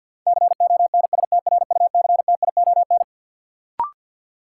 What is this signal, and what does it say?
Morse code, heard as a keyed tone, about 53 wpm, 700 Hz. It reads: CONSTRUCTION
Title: Sentences from Top 1600 Words - New Word Drill - 50wpm